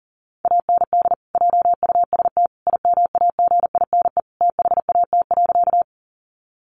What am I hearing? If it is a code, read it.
Morse code, heard as a keyed tone, about 40 wpm, 700 Hz. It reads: AND JUST IMAGINE THAT.
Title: Sentences from Top 1400 Words 40wpm